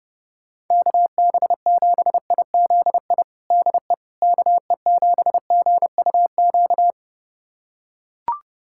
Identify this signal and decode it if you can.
Morse code, heard as a keyed tone, about 30 wpm, 700 Hz. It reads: KB7IZI DE KE7GUQ
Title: USA Callsign DE Callsign - 6 Characters - 30wpm